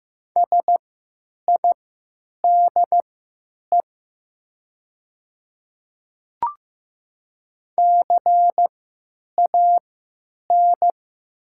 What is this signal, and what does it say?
Morse code, heard as a keyed tone, about 15 wpm, spaced as for 9 wpm, 700 Hz. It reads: SIDE CAN
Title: 3x Character Spacing - Top 100 Words - 15wpm